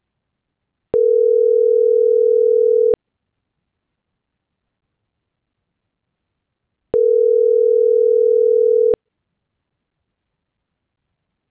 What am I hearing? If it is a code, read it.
Ringback tone (North America), 440 + 480 Hz.